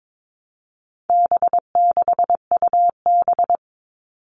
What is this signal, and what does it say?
Morse code, heard as a keyed tone, about 22 wpm, 700 Hz. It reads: B6UB
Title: All Cut Numbers - 4-Digit 22wpm